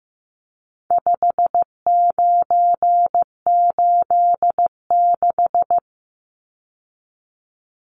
Morse code, heard as a keyed tone, about 15 wpm, 700 Hz. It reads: 5986
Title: Numbers - 4-Digit 15wpm